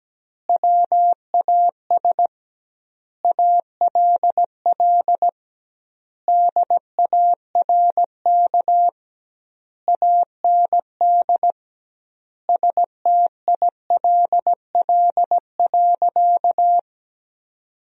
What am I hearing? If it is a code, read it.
Morse code, heard as a keyed tone, about 17 wpm, 700 Hz. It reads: WAS ALL DARK AND STILL.
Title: 2x Word-Spacing - Sentences from Top 500 Words - 17wpm